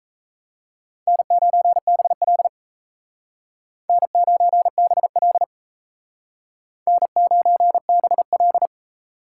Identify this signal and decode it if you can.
Morse code, heard as a keyed tone, about 35 wpm, 700 Hz. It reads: N9BL N9BL N9BL
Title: Speed Racing - USA 1x2 Call Signs 28wpm